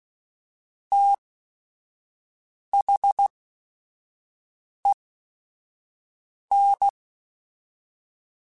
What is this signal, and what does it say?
Morse code, heard as a keyed tone, about 16 wpm, spaced as for 5 wpm, 800 Hz. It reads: THEN
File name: Technical Subjects / Technical Subjects-mp3s / XL @5